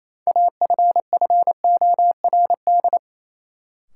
Morse code, heard as a keyed tone, about 28 wpm, 700 Hz. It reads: AFFORD
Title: Sentences from Top 1100 Words - New Word Drill - 28wpm